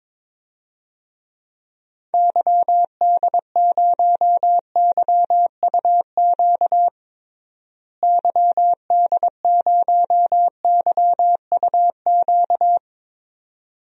Morse code, heard as a keyed tone, about 22 wpm, 700 Hz. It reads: YD0YUQ YD0YUQ
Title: DXCC-100 Call Signs - 6 Characters - Repeated 2x - 22wpm